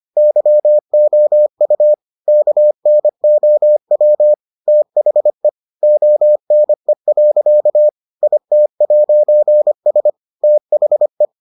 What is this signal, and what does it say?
Morse code, heard as a keyed tone, about 25 wpm, 600 Hz. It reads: YOU KNOW THE ONE. IT'S THE